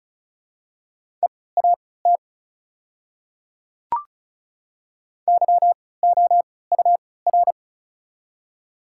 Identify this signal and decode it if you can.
Morse code, heard as a keyed tone, about 34 wpm, spaced as for 20 wpm, 700 Hz. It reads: EAT YOUR
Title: 3x Character Spacing - Top 500 Words - 35wpm